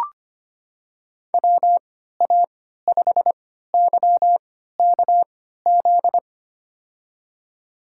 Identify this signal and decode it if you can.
Morse code, heard as a keyed tone, about 25 wpm, spaced as for 14 wpm, 700 Hz. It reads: WA5YKZ
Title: USA Call Signs - 3x Character Spacing - 25wpm